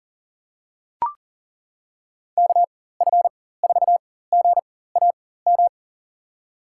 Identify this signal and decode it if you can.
Morse code, heard as a keyed tone, about 40 wpm, spaced as for 19 wpm, 700 Hz. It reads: KF4GAM
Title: USA Call Signs - 6 Characters - 4x Character Spacing - 40wpm